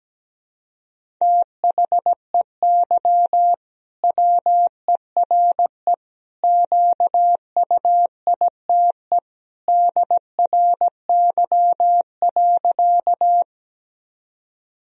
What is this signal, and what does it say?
Morse code, heard as a keyed tone, about 17 wpm, 700 Hz. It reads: THEY WERE QUITE DRY.
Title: Sentences from Top 1000 Words - Review Part 3 - 17wpm